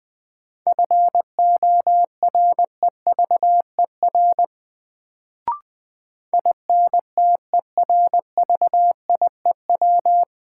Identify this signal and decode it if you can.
Morse code, heard as a keyed tone, about 20 wpm, 700 Hz. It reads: FOREVER INTERVIEW
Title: Sentences from Top 1600 Words - New Word Drill Alt - 20wpm